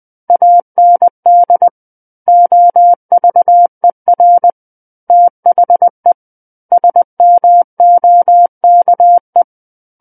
Morse code, heard as a keyed tone, about 20 wpm, 700 Hz. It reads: AND OVER THE SMOKE
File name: 20 WPM